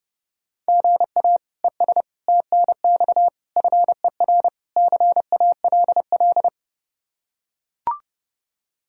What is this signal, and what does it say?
Morse code, heard as a keyed tone, about 30 wpm, 700 Hz. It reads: GA ES TNX FER CALL
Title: Practice QSO - Exchange 1 - Beginning - 30wpm